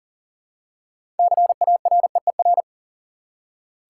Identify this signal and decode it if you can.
Morse code, heard as a keyed tone, about 40 wpm, 700 Hz. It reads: CAREER
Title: Six Letter Words - 40wpm